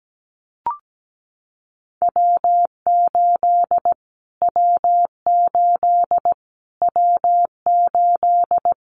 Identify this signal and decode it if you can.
Morse code, heard as a keyed tone, about 17 wpm, 700 Hz. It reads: W8 W8 W8
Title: Callsign Prefixes for Top 25 DXCC - Repeated 3x - 17wpm